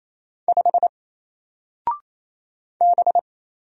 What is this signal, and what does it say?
Morse code, heard as a keyed tone, about 28 wpm, 700 Hz. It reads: 5 B
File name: Single-Letter-Number-28wpm